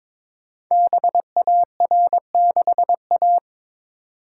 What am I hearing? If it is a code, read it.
Morse code, heard as a keyed tone, about 23 wpm, 690 Hz. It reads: BAR6A
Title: Code Groups - 5 Letter:Numbers - 22wpm